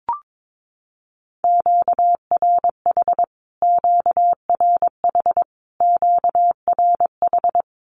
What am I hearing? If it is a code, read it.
Morse code, heard as a keyed tone, about 22 wpm, 700 Hz. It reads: QRH QRH QRH
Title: Q-Codes Repeated 3x 22wpm